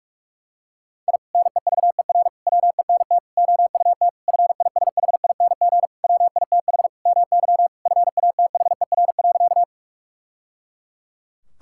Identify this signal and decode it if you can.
Morse code, heard as a keyed tone, about 41 wpm, 700 Hz. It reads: I NEVER WENT OUT FISHING WITH MY FATHER.
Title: Sentences from Top 1000 Words - Review Part 5 - 45wpm